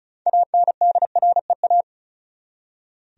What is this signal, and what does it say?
Morse code, heard as a keyed tone, about 36 wpm, 700 Hz. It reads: ANDREA
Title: USA Common Names - 35wpm